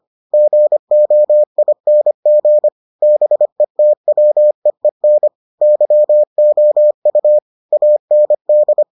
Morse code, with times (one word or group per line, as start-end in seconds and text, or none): 0.3-2.7 GOING
3.0-5.3 BETWEEN
5.6-7.4 YOU
7.7-8.8 AND